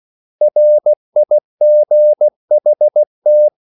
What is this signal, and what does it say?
Morse code, heard as a keyed tone, about 16 wpm, 600 Hz. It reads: RIGHT
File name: CW / Practice / 500MostCommonEnglishWords16